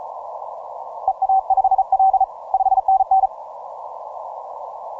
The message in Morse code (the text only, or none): EA5L 5NN